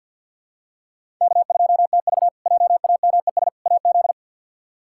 Morse code, heard as a keyed tone, about 49 wpm, 700 Hz. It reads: K2TV JAMES AZ
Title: K1USN SST QSO Practice - 50wpm